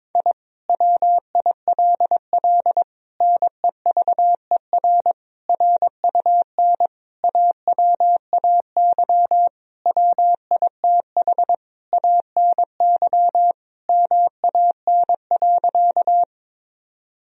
Morse code, heard as a keyed tone, about 22 wpm, 700 Hz. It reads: I WILL NEVER RUN AWAY WITH ANY MAN.
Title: Sentences from Top 1000 Words - Review Part 1 - 22wpm